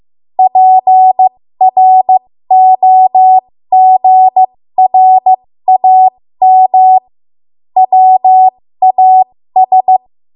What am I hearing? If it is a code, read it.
Morse code, heard as a keyed tone, about 15 wpm, spaced as for 13 wpm, 750 Hz. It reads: PROGRAM WAS
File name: Morse / 13WPM